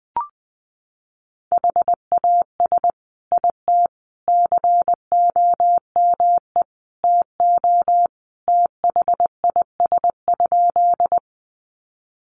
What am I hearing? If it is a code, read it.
Morse code, heard as a keyed tone, about 20 wpm, 700 Hz. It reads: HAS IT COME TO THIS?
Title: Sentences from Top 100 Words 20wpm